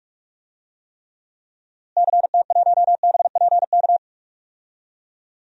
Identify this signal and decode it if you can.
Morse code, heard as a keyed tone, about 46 wpm, 700 Hz. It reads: CT1BPK
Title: DXCC-100 Call Signs - 6 Characters - Repeated 3x - 45wpm